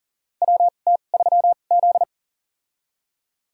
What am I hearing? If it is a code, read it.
Morse code, heard as a keyed tone, about 40 wpm, spaced as for 29 wpm, 700 Hz. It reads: WT3Z
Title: USA Call Signs - 4 Characters - 2x Character Spacing - 40wpm